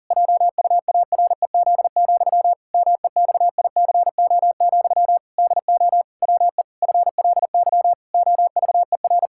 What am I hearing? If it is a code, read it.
Morse code, heard as a keyed tone, about 40 wpm, 700 Hz. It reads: JUAREZ, MEXICO, DO WE FLY OVER
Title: Pangram Challenge v2 - 40wpm